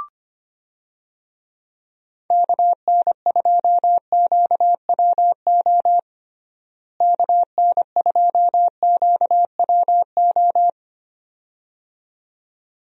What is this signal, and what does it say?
Morse code, heard as a keyed tone, about 25 wpm, 700 Hz. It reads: KN2QWO KN2QWO
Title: USA Call Signs - 6 Characters - Repeated 2x - 25wpm